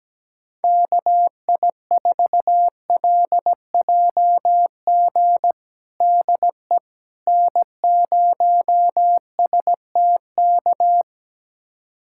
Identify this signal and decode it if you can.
Morse code, heard as a keyed tone, about 17 wpm, 700 Hz. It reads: KI4LJG DE N0STK
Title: USA Callsign DE Callsign 17wpm